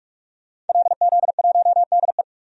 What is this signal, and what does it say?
Morse code, heard as a keyed tone, about 44 wpm, 690 Hz. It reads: LZ1DE
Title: DXCC-100 Call Signs - 5 Characters - 45wpm